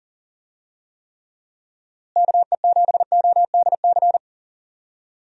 Morse code, heard as a keyed tone, about 40 wpm, 700 Hz. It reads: KE7ODC
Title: USA Call Signs - 6 Characters - Repeated 3x - 40wpm